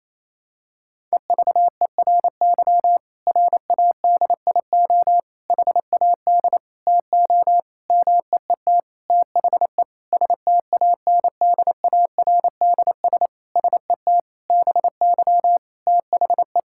Morse code, heard as a keyed tone, about 28 wpm, 700 Hz. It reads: EVERY RADIO HAD TO MEET THE STANDARDS SET BY THE
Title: Sentences from Top 1000 Words - Review Part 6 - 28wpm